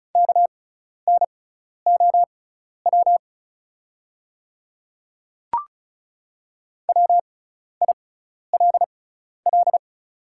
Morse code, heard as a keyed tone, about 36 wpm, spaced as for 12 wpm, 700 Hz. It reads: KNOW WILL
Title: 6x Character Spacing - Top 100 Words - 35wpm